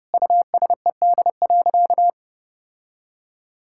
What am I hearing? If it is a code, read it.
Morse code, heard as a keyed tone, about 30 wpm, 700 Hz. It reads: USED.
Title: Sentences from Top 1600 Words 30wpm